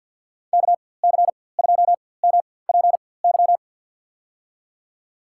Morse code, heard as a keyed tone, about 51 wpm, spaced as for 23 wpm, 700 Hz. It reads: KC2MPY
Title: USA Call Signs - 6 Characters - 4x Character Spacing - 50wpm